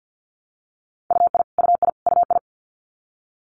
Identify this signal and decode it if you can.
Morse code, heard as a keyed tone, about 50 wpm, 700 Hz. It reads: UI UI UI
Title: Multi-letter Phonemes Repeated 3x 50wpm